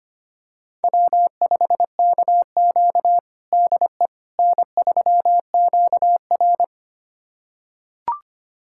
Morse code, heard as a keyed tone, about 25 wpm, 700 Hz. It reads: W5KQ DE N3QR
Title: USA Callsign DE Callsign - 4 Characters - 25wpm